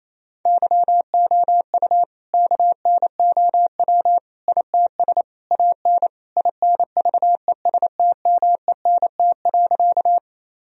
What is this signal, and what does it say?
Morse code, heard as a keyed tone, about 28 wpm, 700 Hz. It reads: YOU KNOW ITS AN INVESTMENT.
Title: Sentences from Top 1000 Words - Review Part 4 - 28wpm